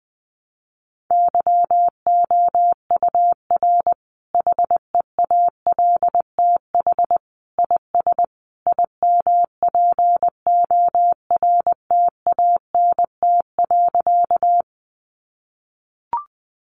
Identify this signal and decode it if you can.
Morse code, heard as a keyed tone, about 20 wpm, 700 Hz. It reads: YOUR HEALTH IS IMPORTANT.
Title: Sentences from Top 600 Words 20wpm